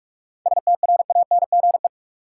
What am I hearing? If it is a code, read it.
Morse code, heard as a keyed tone, about 47 wpm, 700 Hz. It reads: STRANGE
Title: Seven Letter Words - 45wpm